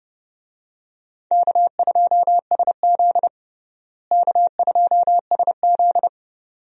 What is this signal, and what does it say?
Morse code, heard as a keyed tone, about 32 wpm, 700 Hz. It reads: K2SZ K2SZ
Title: USA Call Signs - 4 Characters - Repeated 2x - 30wpm